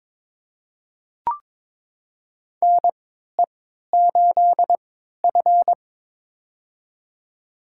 Morse code, heard as a keyed tone, about 22 wpm, spaced as for 12 wpm, 700 Hz. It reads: NE8F
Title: USA Call Signs - 4 Characters - 3x Character Spacing - 22wpm